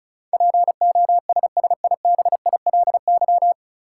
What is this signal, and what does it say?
Morse code, heard as a keyed tone, about 34 wpm, 700 Hz. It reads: POSSIBILY
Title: Sentences from Top 1300 Words - New Word Drill Alt - 35wpm